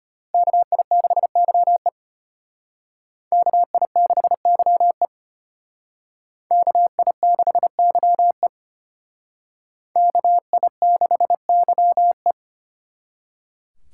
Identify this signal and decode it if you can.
Morse code, heard as a keyed tone, about 32 wpm, 700 Hz. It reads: KI6YE KI6YE KI6YE KI6YE